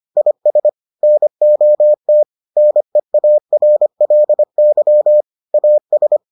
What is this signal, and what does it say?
Morse code, heard as a keyed tone, about 25 wpm, 600 Hz. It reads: IS NOT NEARLY AS